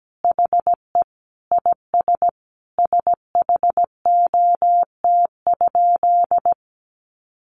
Morse code, heard as a keyed tone, about 17 wpm, 700 Hz. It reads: HE IS SHOT?
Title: Sentences from Top 1100 Words 17wpm